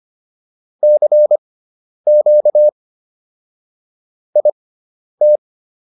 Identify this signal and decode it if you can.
Morse code, heard as a keyed tone, about 25 wpm, spaced as for 10 wpm, 600 Hz. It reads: CQ IT